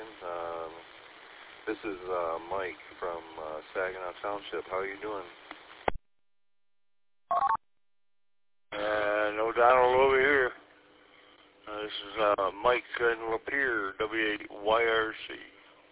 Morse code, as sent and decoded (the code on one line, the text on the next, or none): none